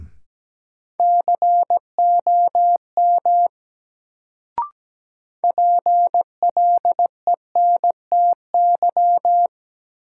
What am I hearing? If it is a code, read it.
Morse code, heard as a keyed tone, about 17 wpm, 700 Hz. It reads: COM PLENTY
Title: Sentences from Top 1300 Words - New Word Drill - 17wpm